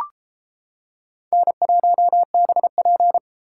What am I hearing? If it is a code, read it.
Morse code, heard as a keyed tone, about 31 wpm, 700 Hz. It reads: N1BP